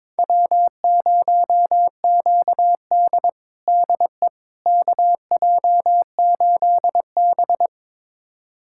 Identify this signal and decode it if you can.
Morse code, heard as a keyed tone, about 22 wpm, 700 Hz. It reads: W0QD DE KJ8B
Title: USA Callsign DE Callsign - 4 Characters - 22wpm